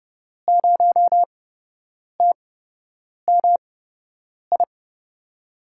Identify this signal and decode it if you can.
Morse code, heard as a keyed tone, about 30 wpm, spaced as for 8 wpm, 700 Hz. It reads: 0TMI
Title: USA Call Signs - 8x Character Spacing - 30wpm